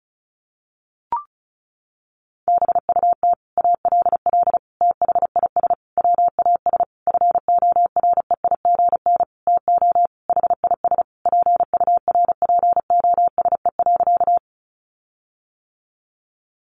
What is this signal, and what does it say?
Morse code, heard as a keyed tone, about 34 wpm, 700 Hz. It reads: BUT ALL THIS WAS FOREIGN TO HIS PURPOSE.
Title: Sentences from Top 1400 Words 35wpm